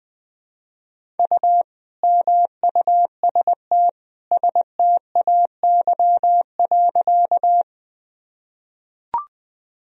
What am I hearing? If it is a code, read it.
Morse code, heard as a keyed tone, about 20 wpm, 700 Hz. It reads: U MUST STAY.
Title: Sentences from Top 500 Words and Abbreviations 20wpm